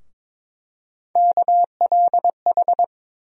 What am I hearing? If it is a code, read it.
Morse code, heard as a keyed tone, about 22 wpm, 700 Hz. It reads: KLH
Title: USA Call Sign Postfixes - 3 Letters 22wpm